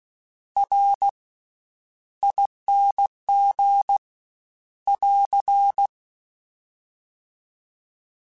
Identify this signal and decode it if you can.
Morse code, heard as a keyed tone, about 16 wpm, 800 Hz. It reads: R ING <AR>
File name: Special INGs-mp3s / Special INGs-mp3s / Slow ing @15